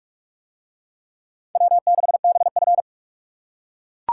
Morse code, heard as a keyed tone, about 40 wpm, 700 Hz. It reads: W6BF